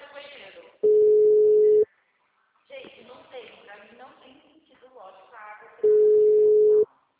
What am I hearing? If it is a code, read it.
Ringback tone (Europe, 425 Hz), with speech.